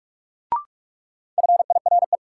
Morse code, heard as a keyed tone, about 46 wpm, 700 Hz. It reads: FIRE
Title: Sentences from Top 400 Words - New Word Drill - 45wpm